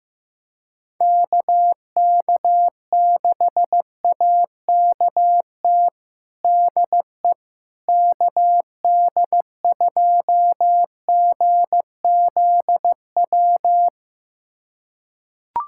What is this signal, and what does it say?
Morse code, heard as a keyed tone, about 15 wpm, 700 Hz. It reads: KK6AKT DE KD2GZW
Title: USA Callsign DE Callsign - 6 Characters - 15wpm